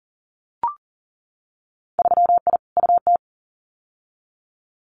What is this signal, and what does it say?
Morse code, heard as a keyed tone, about 40 wpm, 700 Hz. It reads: 3I UT
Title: Winter Field Day Exchanges Encore 40wpm